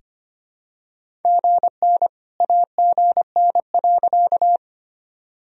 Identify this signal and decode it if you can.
Morse code, heard as a keyed tone, about 25 wpm, 700 Hz. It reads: GN AGN.
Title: Sentences from Top 300 Words and Abbreviations 25wpm